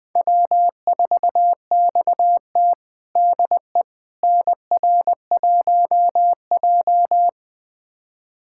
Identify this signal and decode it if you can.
Morse code, heard as a keyed tone, about 20 wpm, 700 Hz. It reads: W4XT DE NR1J
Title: USA Callsign DE Callsign - 4 Characters - 20wpm